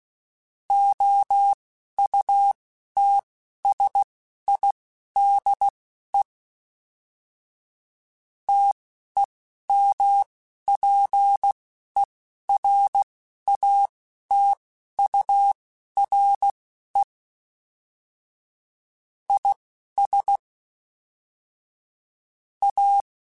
Morse code, heard as a keyed tone, about 16 wpm, spaced as for 12 wpm, 800 Hz. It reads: OUTSIDE TEMPERATURE IS A